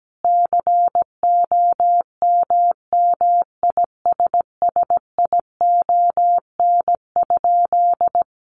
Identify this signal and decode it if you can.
Morse code, heard as a keyed tone, about 17 wpm, 700 Hz. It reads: COMMISSION?
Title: Sentences from Top 1400 Words 17wpm